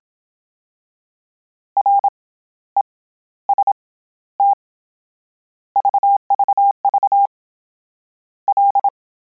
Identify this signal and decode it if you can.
Morse code, heard as a keyed tone, about 27 wpm, spaced as for 14 wpm, 800 Hz. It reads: REST VVV L